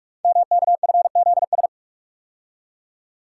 Morse code, heard as a keyed tone, about 44 wpm, 690 Hz. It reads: MKFZS
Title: Code Groups - 5 Letters - 45wpm